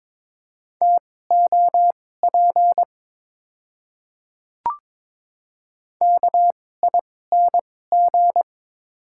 Morse code, heard as a keyed tone, about 22 wpm, spaced as for 16 wpm, 700 Hz. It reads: TOP KING
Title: 2x Character Spacing - Top 500 Words - 22wpm